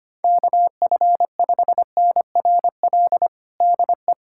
Morse code, heard as a keyed tone, about 25 wpm, 700 Hz. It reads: KF5NRL DE
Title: USA Callsign DE Callsign 25wpm